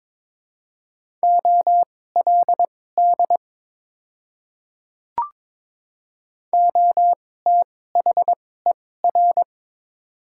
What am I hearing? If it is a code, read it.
Morse code, heard as a keyed tone, about 22 wpm, spaced as for 16 wpm, 700 Hz. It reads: OLD OTHER
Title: 2x Character Spacing - Top 200 Words - 22wpm